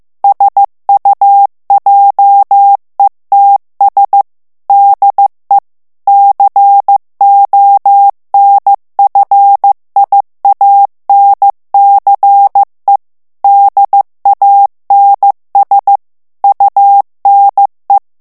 Morse code, heard as a keyed tone, about 15 wpm, 800 Hz. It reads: SUJETS DE CONFIANCE DANS UNE
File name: morse mp3 / FRANCAIS TEXTE CLAIR 2 VITESSE 14 MOTS MINUTES